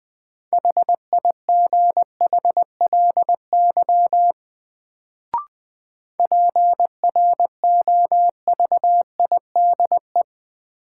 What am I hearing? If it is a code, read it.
Morse code, heard as a keyed tone, about 20 wpm, 700 Hz. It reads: HIGHLY PROVIDE
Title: Sentences from Top 1100 Words - New Word Drill Alt - 20wpm